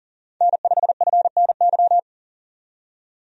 Morse code, heard as a keyed tone, about 40 wpm, 700 Hz. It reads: N5FNY
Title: USA Call Signs - 5 Characters - 40wpm